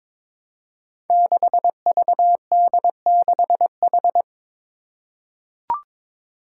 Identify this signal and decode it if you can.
Morse code, heard as a keyed tone, about 22 wpm, 700 Hz. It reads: 6VD6H
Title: Five Challenging Letter-Number Combinations 22wpm